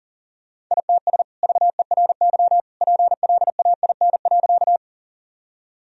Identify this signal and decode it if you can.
Morse code, heard as a keyed tone, about 40 wpm, 700 Hz. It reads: ITS VERY PLAIN.
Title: Sentences from Top 1000 Words - Review Part 2 - 40wpm